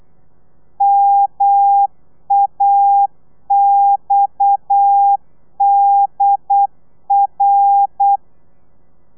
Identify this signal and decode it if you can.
Morse code, heard as a keyed tone, about 8 wpm, 800 Hz. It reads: MAXDR
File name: Lez11